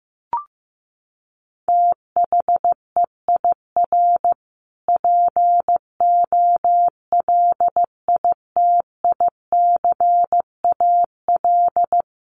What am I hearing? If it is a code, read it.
Morse code, heard as a keyed tone, about 15 wpm, 700 Hz. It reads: THEIR POLITICAL
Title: Sentences from Top 1300 Words 15wpm